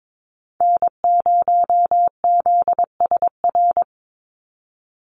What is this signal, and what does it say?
Morse code, heard as a keyed tone, about 22 wpm, 700 Hz. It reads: N0ZSR
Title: USA Call Signs 22wpm